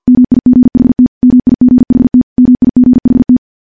Morse code, heard as a keyed tone, about 50 wpm, 250 Hz. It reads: MIGHT MIGHT MIGHT
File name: morse-training-50WPM